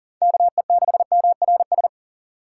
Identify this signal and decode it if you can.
Morse code, heard as a keyed tone, about 40 wpm, 700 Hz. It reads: KE6MRS
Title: USA Call Signs - 6 Characters - 4x Character Spacing - 40wpm